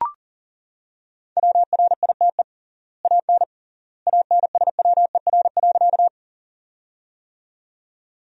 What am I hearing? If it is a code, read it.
Morse code, heard as a keyed tone, about 40 wpm, 700 Hz. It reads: WRITE AN ANSWER.
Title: 3x Word-Spacing - Sentences from Top 300 Words - 40wpm